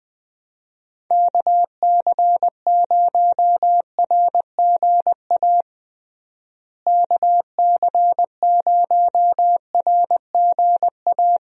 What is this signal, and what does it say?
Morse code, heard as a keyed tone, about 21 wpm, 700 Hz. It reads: KC0RGA KC0RGA
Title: USA Call Signs - 6 Characters - Repeated 2x - 20wpm